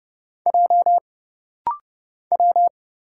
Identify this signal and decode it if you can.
Morse code, heard as a keyed tone, about 30 wpm, 700 Hz. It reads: J W
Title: Single Letter-Number Alt - Rapid-Fire 30wpm